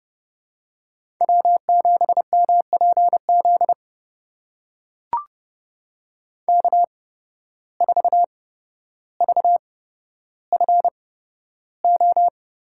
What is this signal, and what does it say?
Morse code, heard as a keyed tone, about 31 wpm, spaced as for 13 wpm, 700 Hz. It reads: W7MPZ K4VFO
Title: USA Call Signs - 5 Characters - 8x Character Spacing - 30wpm